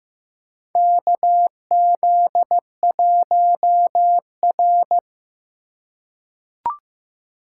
Morse code, heard as a keyed tone, about 15 wpm, 700 Hz. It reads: KZ1R
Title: USA Call Signs - 2x Character Spacing - 15wpm